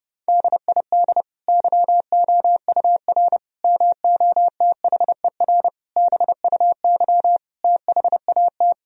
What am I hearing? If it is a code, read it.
Morse code, heard as a keyed tone, about 30 wpm, 700 Hz. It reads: DID YOUR MOTHER BUY THAT
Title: Sentences from Top 1000 Words - Review Part 3 - 30wpm